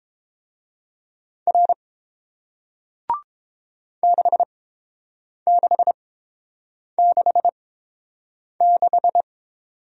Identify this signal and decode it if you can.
Morse code, heard as a keyed tone, about 30 wpm, spaced as for 7 wpm, 700 Hz. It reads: R 6666